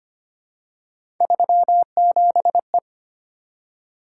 Morse code, heard as a keyed tone, about 25 wpm, 700 Hz. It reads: 37E
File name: Three Letter-Number Combinations - 25wpm